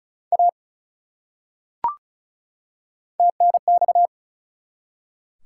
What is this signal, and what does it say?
Morse code, heard as a keyed tone, about 36 wpm, 700 Hz. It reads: A TNX
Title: Top 180 QSO Elements 35wpm